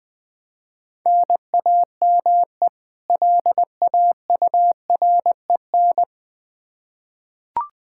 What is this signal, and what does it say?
Morse code, heard as a keyed tone, about 21 wpm, 700 Hz. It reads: NAME LAUREN
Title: K1USN SST QSO Practice 20wpm